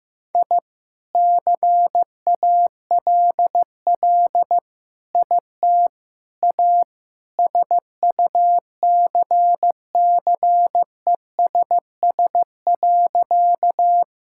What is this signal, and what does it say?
Morse code, heard as a keyed tone, about 15 wpm, 700 Hz. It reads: I CALL IT A SUCCESS.
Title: Sentences from Top 1000 Words - Review Part 6 - 15wpm